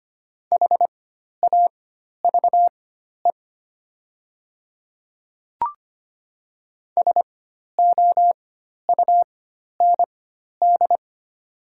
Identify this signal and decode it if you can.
Morse code, heard as a keyed tone, about 25 wpm, spaced as for 12 wpm, 700 Hz. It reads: HAVE SOUND
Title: 4x Character Spacing - Top 200 Words - 25wpm